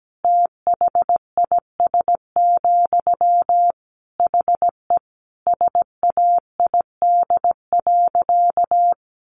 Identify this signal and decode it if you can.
Morse code, heard as a keyed tone, about 17 wpm, 700 Hz. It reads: THIS, HE SAID.
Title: Sentences from Top 400 Words 17wpm